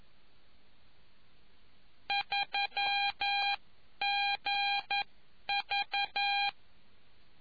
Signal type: Morse code